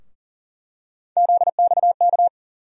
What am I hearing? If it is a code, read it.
Morse code, heard as a keyed tone, about 40 wpm, 700 Hz. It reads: ZXK
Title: USA Call Sign Postfixes - 3 Letters 40wpm